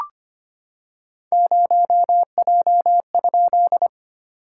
Morse code, heard as a keyed tone, about 25 wpm, 700 Hz. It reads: 0J?
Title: SOTA QSO Practice 25wpm